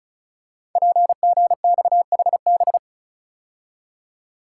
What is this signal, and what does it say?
Morse code, heard as a keyed tone, about 36 wpm, 700 Hz. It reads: PGXHB
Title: Code Groups - 5 Letters - 35wpm